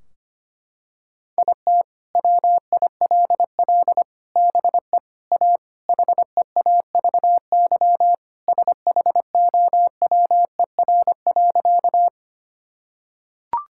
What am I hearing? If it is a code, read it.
Morse code, heard as a keyed tone, about 25 wpm, 700 Hz. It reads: IT WILL BE A HEAVY SHOWER.
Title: Sentences from Top 1600 Words 25wpm